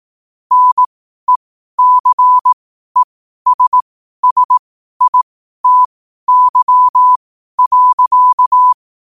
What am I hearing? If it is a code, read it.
Morse code, heard as a keyed tone, about 18 wpm, spaced as for 12 wpm, 1000 Hz. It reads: NECESSITY.
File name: qotdfiles / 12wpm